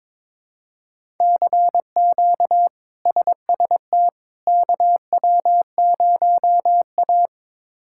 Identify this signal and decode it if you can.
Morse code, heard as a keyed tone, about 22 wpm, 700 Hz. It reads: CQ SST KW0A
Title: K1USN SST QSO Practice - 22wpm